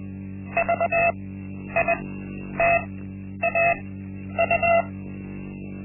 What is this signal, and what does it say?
Morse code, heard as a keyed tone, about 20 wpm, spaced as for 10 wpm, 680 Hz. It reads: VITAU
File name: Linksmakalnis-1